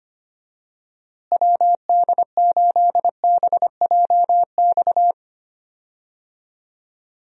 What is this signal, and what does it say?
Morse code, heard as a keyed tone, about 25 wpm, 700 Hz. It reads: WD8BJX
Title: USA Call Signs - 6 Characters - Repeated 3x - 25wpm